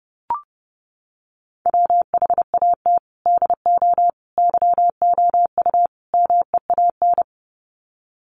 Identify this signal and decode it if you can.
Morse code, heard as a keyed tone, about 28 wpm, 700 Hz. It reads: WHAT DO YOU MEAN
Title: Sets of 4 Words 30wpm